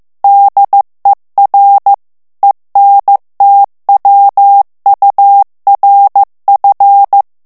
Morse code, heard as a keyed tone, about 15 wpm, 800 Hz. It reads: DER ENTWURF
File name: ALLEMAND texte clair 1 vitesse 14 mots minute